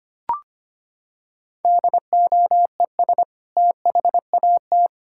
Morse code, heard as a keyed tone, about 25 wpm, 700 Hz. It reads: DOES THAT
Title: Sentences from Top 1600 Words 25wpm